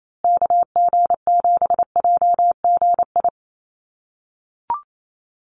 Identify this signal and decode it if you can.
Morse code, heard as a keyed tone, about 28 wpm, 700 Hz. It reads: KG7JGI
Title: USA Call Signs Repeated 3x 28wpm